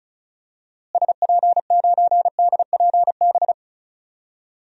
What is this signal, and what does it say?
Morse code, heard as a keyed tone, about 36 wpm, 700 Hz. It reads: SP9DPB
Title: DXCC-100 Call Signs - 6 Characters - Repeated 3x - 35wpm